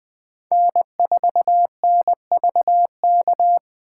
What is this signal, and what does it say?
Morse code, heard as a keyed tone, about 20 wpm, 700 Hz. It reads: N4NVK
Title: Speed Racing - USA 1x3 Call Signs 17wpm